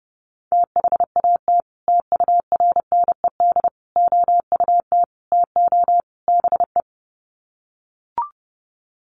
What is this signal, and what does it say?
Morse code, heard as a keyed tone, about 30 wpm, 700 Hz. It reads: THAT TURNED OUT TO BE
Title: Sets of 5 Words Encore2 30wpm